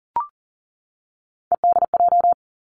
Morse code, heard as a keyed tone, about 38 wpm, 690 Hz. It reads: EDJ
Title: Three Challenging Letter-Number Combinations 40wpm